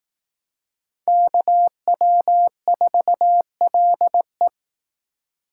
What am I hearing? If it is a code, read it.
Morse code, heard as a keyed tone, about 18 wpm, 700 Hz. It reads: KW4LE
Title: Speed Racing - USA 2x2 Call Signs 15wpm